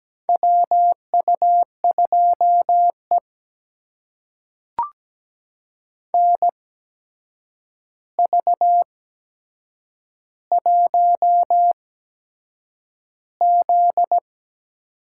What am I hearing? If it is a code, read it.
Morse code, heard as a keyed tone, about 17 wpm, spaced as for 7 wpm, 700 Hz. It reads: WU2E NV1Z
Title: USA Call Signs - 4 Characters - 8x Character Spacing - 17wpm